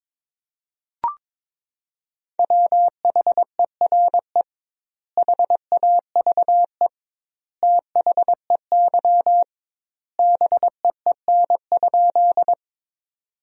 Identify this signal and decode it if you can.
Morse code, heard as a keyed tone, about 23 wpm, 700 Hz. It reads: WHERE HAVE THEY BEEN?
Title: 2x Word-Spacing - Sentences from Top 200 Words - 22wpm